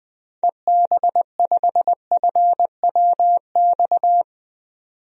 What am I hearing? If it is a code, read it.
Morse code, heard as a keyed tone, about 20 wpm, 700 Hz. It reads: EB5FWX
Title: DXCC-100 Call Signs - 6 Characters - 20wpm